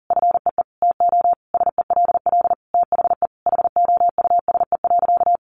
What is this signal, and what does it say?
Morse code, heard as a keyed tone, about 40 wpm, 700 Hz. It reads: FEE TO SELL THE HOUSE.
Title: Sentences from Top 1500 Words 40wpm